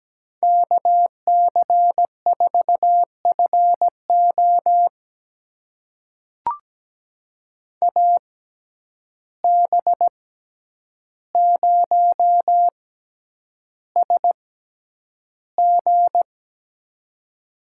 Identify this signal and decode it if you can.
Morse code, heard as a keyed tone, about 17 wpm, spaced as for 9 wpm, 700 Hz. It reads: KC4FO AB0SG
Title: USA Call Signs - 5 Characters - 6x Character Spacing - 17wpm